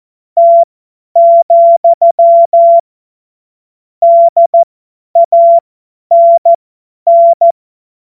Morse code, heard as a keyed tone, about 14 wpm, spaced as for 10 wpm, 680 Hz. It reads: T, DANN
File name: Morsekurs / Witze / Witz 20. Januar 2026 50 F70